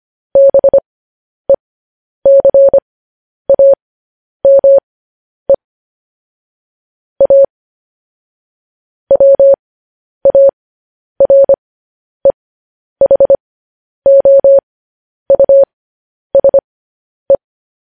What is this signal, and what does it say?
Morse code, heard as a keyed tone, about 25 wpm, spaced as for 10 wpm, 550 Hz. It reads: BECAME A WAREHOUSE
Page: news headlines in morse code at 10 wpm